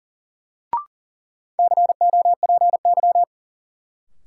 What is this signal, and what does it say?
Morse code, heard as a keyed tone, about 40 wpm, 700 Hz. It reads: COPY
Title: Sentences from Top 1400 Words - New Word Drill - 40wpm